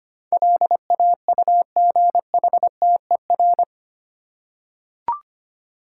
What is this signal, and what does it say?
Morse code, heard as a keyed tone, about 25 wpm, 700 Hz. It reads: LAUGHTER
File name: Eight Letter Words - 25wpm